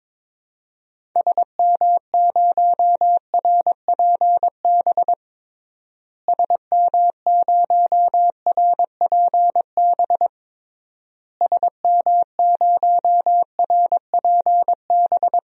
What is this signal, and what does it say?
Morse code, heard as a keyed tone, about 22 wpm, 700 Hz. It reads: SM0RPB SM0RPB SM0RPB
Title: DXCC-100 Call Signs - 6 Characters - Repeated 3x - 22wpm